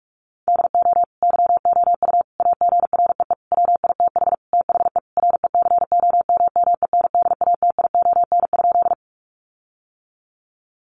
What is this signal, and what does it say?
Morse code, heard as a keyed tone, about 44 wpm, 700 Hz. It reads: DO YOU AGREE WITH THE RECOMMENDATION?
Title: Sentences from Top 1400 Words 45wpm